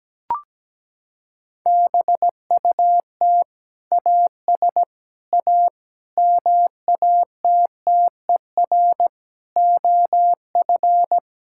Morse code, heard as a keyed tone, about 17 wpm, 700 Hz. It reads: BUT AS A MATTER OF
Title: Sentences from Top 1000 Words - Review Part 6 - 17wpm